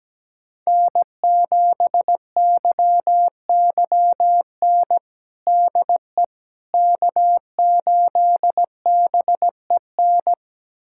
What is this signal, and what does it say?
Morse code, heard as a keyed tone, about 17 wpm, 700 Hz. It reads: N7YYN DE K8BEN
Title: USA Callsign DE Callsign - 5 Characters - 17wpm